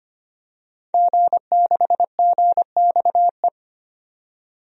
Morse code, heard as a keyed tone, about 25 wpm, 700 Hz. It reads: G6GXE
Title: DXCC-100 Call Signs - 5 Characters - 25wpm